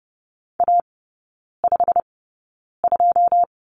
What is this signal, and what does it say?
Morse code, heard as a keyed tone, about 30 wpm, spaced as for 9 wpm, 700 Hz. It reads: A52
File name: Lesson 18e - Letter m - Three Characters 30wpm